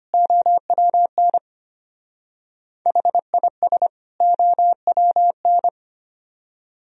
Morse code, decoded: OWN HIS OWN

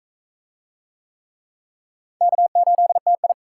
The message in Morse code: K8TI

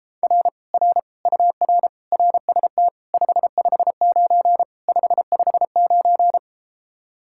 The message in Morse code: R R UR RST 559 559

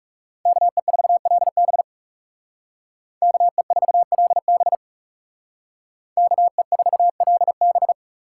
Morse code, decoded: KE4LB KE4LB KE4LB

